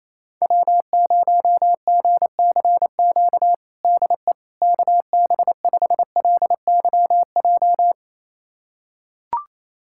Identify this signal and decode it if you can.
Morse code, heard as a keyed tone, about 28 wpm, 700 Hz. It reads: W0GCQ DE KB5LYJ
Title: USA Callsign DE Callsign 28wpm